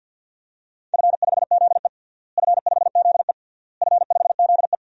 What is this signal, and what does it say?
Morse code, heard as a keyed tone, about 50 wpm, 700 Hz. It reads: F5ZE F5ZE F5ZE
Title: DXCC-100 Call Signs - 4 Characters - Repeated 3x - 50wpm